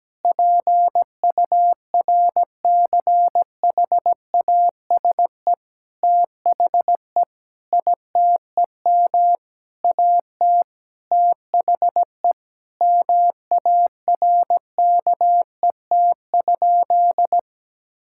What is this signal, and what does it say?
Morse code, heard as a keyed tone, about 17 wpm, 700 Hz. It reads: PURCHASE THE ITEM AT THE MARKET?
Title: Sentences from Top 1000 Words - Review Part 6 - 17wpm